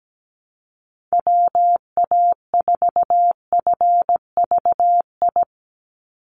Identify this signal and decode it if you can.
Morse code, heard as a keyed tone, about 17 wpm, 700 Hz. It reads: WA4FVI
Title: USA Call Signs 17wpm